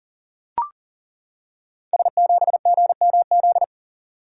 Morse code, heard as a keyed tone, about 37 wpm, 700 Hz. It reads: S7GMZ
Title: Code Groups - 5 Letter:Numbers - 40wpm